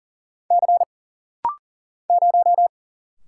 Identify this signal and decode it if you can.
Morse code, heard as a keyed tone, about 40 wpm, 700 Hz. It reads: C 0
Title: Single Letter-Number - Rapid-Fire 40wpm